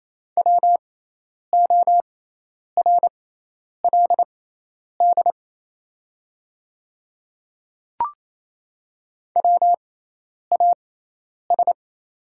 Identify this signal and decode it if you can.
Morse code, heard as a keyed tone, about 28 wpm, spaced as for 10 wpm, 700 Hz. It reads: WORLD WAS
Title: 6x Character Spacing - Top 200 Words - 28wpm